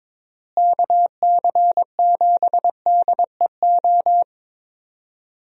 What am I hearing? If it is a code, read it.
Morse code, heard as a keyed tone, about 22 wpm, 700 Hz. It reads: KC7DEO
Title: USA Call Signs - 6 Characters - Repeated 2x - 22wpm